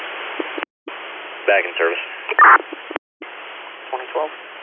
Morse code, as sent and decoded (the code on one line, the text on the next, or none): none